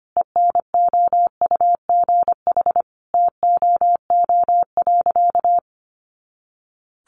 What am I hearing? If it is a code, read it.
Morse code, heard as a keyed tone, about 25 wpm, 700 Hz. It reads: ENOUGH TOO.
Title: Additional Easy Phrases 25wpm